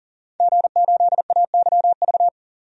Morse code, heard as a keyed tone, about 37 wpm, 700 Hz. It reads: G8AYV
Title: DXCC-100 Call Signs - 5 Characters - 40wpm